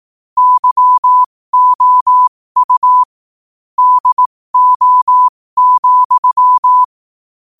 Morse code, heard as a keyed tone, about 18 wpm, spaced as for 15 wpm, 1000 Hz. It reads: YOU DO,
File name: qotdfiles / 15wpm